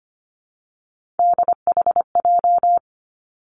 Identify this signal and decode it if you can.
Morse code, heard as a keyed tone, about 25 wpm, 700 Hz. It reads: DHJ
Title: Three Challenging Letter-Number Combinations 25wpm